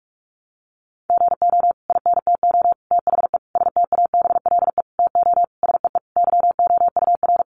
Morse code, heard as a keyed tone, about 44 wpm, 700 Hz. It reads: GO INTO THE STABLE TO SEE YOUR
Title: Sentences from Top 1500 Words 45wpm